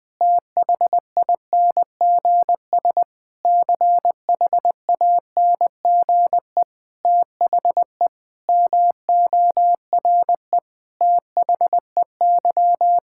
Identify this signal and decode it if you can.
Morse code, heard as a keyed tone, about 21 wpm, 700 Hz. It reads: THINGS CHANGE THE MORE THEY
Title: Sets of 8 Words - 20wpm